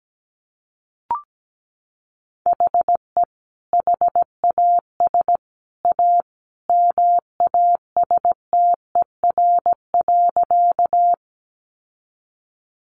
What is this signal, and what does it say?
Morse code, heard as a keyed tone, about 17 wpm, 700 Hz. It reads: HE HAS A MASTER.
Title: Sentences from Top 1300 Words 17wpm